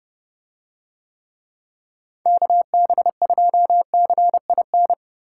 Morse code, heard as a keyed tone, about 29 wpm, 700 Hz. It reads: KB2CIN